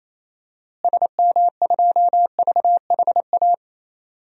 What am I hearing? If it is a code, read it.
Morse code, heard as a keyed tone, about 28 wpm, 700 Hz. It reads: SM2VHA
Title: DXCC-100 Call Signs - 6 Characters - Repeated 2x - 28wpm